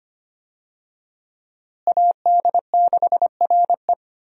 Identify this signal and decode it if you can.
Morse code, heard as a keyed tone, about 25 wpm, 700 Hz. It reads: AD6RE
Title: USA Call Signs - 5 Characters - Repeated 3x - 25wpm